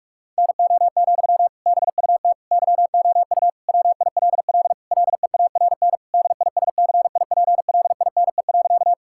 Morse code, heard as a keyed tone, about 46 wpm, 700 Hz. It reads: NO, BUT YOU WILL LEARN DISCIPLINE.
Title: Sentences from Top 1000 Words - Review Part 6 - 45wpm